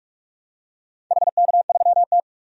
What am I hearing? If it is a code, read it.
Morse code, heard as a keyed tone, about 44 wpm, 700 Hz. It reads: HK3T
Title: DXCC-100 Call Signs - 4 Characters - 45wpm